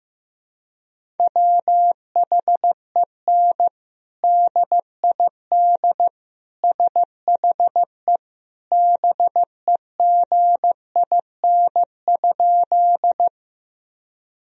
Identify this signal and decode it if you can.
Morse code, heard as a keyed tone, about 15 wpm, 700 Hz. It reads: WHEN DID SHE BEGIN?
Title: Sentences from Top 1000 Words - Review Part 1 - 15wpm